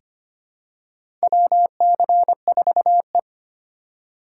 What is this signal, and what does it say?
Morse code, heard as a keyed tone, about 25 wpm, 700 Hz. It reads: WC4E
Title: USA Call Signs - 4 Characters - 4x Character Spacing - 25wpm